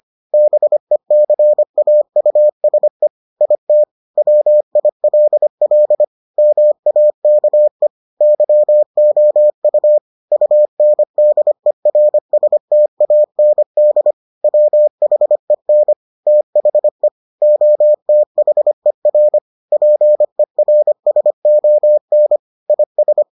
Morse code, heard as a keyed tone, about 25 wpm, 600 Hz. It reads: BECAUSE IT WILL MAKE YOU UNDERSTAND WHEN THE OTHER PERSON IS